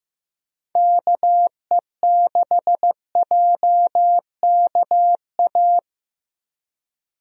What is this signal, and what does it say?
Morse code, heard as a keyed tone, about 15 wpm, 700 Hz. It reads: KE6JKA
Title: USA Call Signs - 6 Characters - Repeated 2x - 15wpm